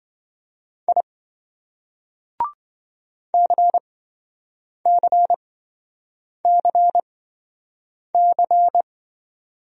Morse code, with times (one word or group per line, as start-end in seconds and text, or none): 0.9-1.0 I
3.3-8.8 CCCC